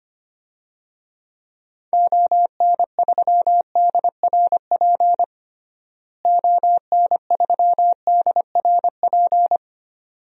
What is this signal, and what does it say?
Morse code, heard as a keyed tone, about 25 wpm, 700 Hz. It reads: ON3DRP ON3DRP
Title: DXCC-100 Call Signs - 6 Characters - Repeated 2x - 25wpm